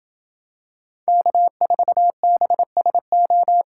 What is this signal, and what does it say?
Morse code, heard as a keyed tone, about 25 wpm, 700 Hz. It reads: K4BSO